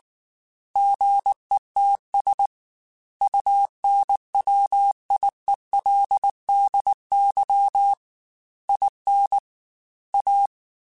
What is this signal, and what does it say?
Morse code, heard as a keyed tone, about 19 wpm, 800 Hz. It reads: GETS UNWIELDY IN A
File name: Technical Subjects / Technical Subjects-mp3s / AND Gate @18